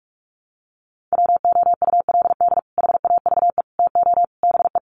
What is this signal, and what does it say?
Morse code, heard as a keyed tone, about 44 wpm, 700 Hz. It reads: WOULD HAVE TO BE
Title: Sets of 4 Words 45wpm